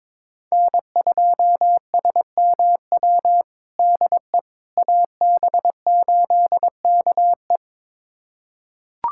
Morse code, heard as a keyed tone, about 22 wpm, 700 Hz. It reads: N2SMW DE AB8KE
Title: USA Callsign DE Callsign - 5 Characters - 22wpm